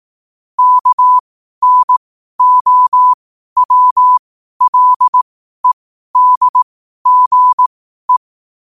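Morse code, heard as a keyed tone, about 18 wpm, spaced as for 12 wpm, 1000 Hz. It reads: KNOWLEDGE
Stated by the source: Morse code quote practice at 12 words per minute.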